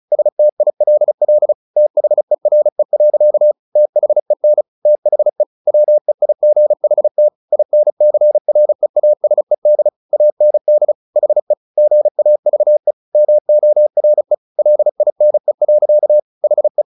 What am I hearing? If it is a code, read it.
Morse code, heard as a keyed tone, about 37 wpm, 600 Hz. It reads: STILL THERE. THEN THE WEIGHT INCREASED AND HE GAVE MORE LINE. HE
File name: the old man and the sea / the old man and the sea 1 35wpm